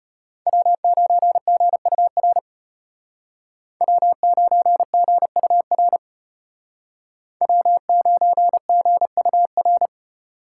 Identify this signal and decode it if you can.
Morse code, heard as a keyed tone, about 33 wpm, 700 Hz. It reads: W9GUR W9GUR W9GUR